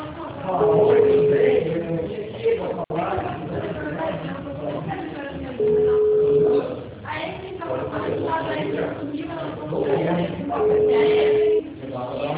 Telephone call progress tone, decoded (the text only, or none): ringback tone